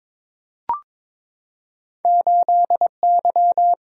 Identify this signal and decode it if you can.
Morse code, heard as a keyed tone, about 23 wpm, 700 Hz. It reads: 8Y
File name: practice / 22wpm / Two Letter-Number Combinations - 22wpm